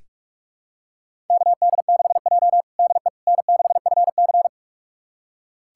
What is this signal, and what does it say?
Morse code, heard as a keyed tone, about 46 wpm, 700 Hz. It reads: KD6J DE N6FC
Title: USA Callsign DE Callsign - 4 Characters - 45wpm